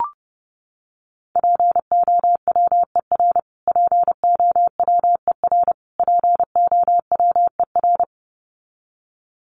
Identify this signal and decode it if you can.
Morse code, heard as a keyed tone, about 30 wpm, 700 Hz. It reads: POWER POWER POWER
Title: Top 500 Words Repeated 30wpm